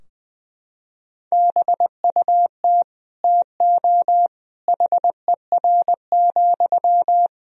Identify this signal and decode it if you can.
Morse code, heard as a keyed tone, about 20 wpm, 700 Hz. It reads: BUT TO HER,
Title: Sentences from Top 1000 Words - Review Part 2 - 20wpm